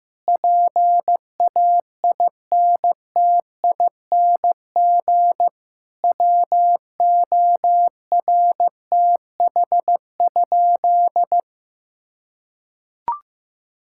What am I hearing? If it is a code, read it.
Morse code, heard as a keyed tone, about 14 wpm, 700 Hz. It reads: PAINTING WORTH?
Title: Sentences from Top 1000 Words - Review Part 6 - 15wpm